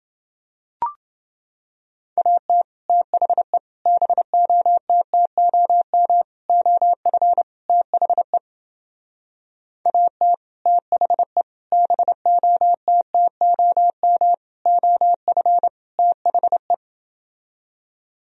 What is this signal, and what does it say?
Morse code, heard as a keyed tone, about 29 wpm, 700 Hz. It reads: AT THE BOTTOM OF THE AT THE BOTTOM OF THE